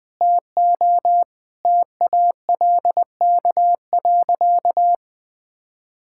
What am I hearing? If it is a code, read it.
Morse code, heard as a keyed tone, about 20 wpm, 700 Hz. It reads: TO TALK.
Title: Sentences with 5-Word Phrases - Spoken Beginning - 20wpm